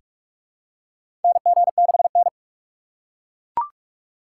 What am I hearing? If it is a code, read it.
Morse code, heard as a keyed tone, about 41 wpm, 700 Hz. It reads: NG6N
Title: USA Call Signs - 4 Characters - 3x Character Spacing - 45wpm